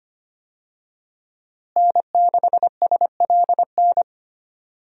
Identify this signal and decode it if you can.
Morse code, heard as a keyed tone, about 25 wpm, 700 Hz. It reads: N6SLN
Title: USA Call Signs - 5 Characters - Repeated 2x - 25wpm